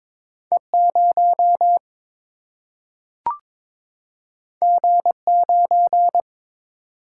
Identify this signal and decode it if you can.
Morse code, heard as a keyed tone, about 22 wpm, 700 Hz. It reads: E 0 G 9
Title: Single-Letter Single-Number 22wpm